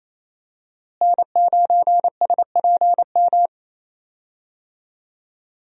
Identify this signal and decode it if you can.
Morse code, heard as a keyed tone, about 28 wpm, 700 Hz. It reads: N9SPM
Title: USA Call Signs - 5 Characters - Repeated 2x - 28wpm